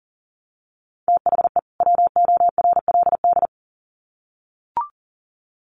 Morse code, heard as a keyed tone, about 40 wpm, 700 Hz. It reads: THE WORLD
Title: Sets of 2 Words 40wpm